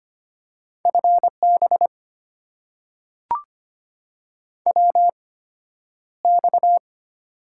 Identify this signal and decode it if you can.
Morse code, heard as a keyed tone, about 25 wpm, spaced as for 11 wpm, 700 Hz. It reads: FB WX
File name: Top 90 QSO Elements - 8x Character Spacing - 25wpm